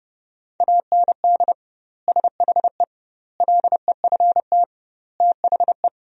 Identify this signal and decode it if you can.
Morse code, heard as a keyed tone, about 30 wpm, 700 Hz. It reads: AND SHE LEFT THE
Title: 2x Word-Spacing - Sentences from Top 300 Words - 30wpm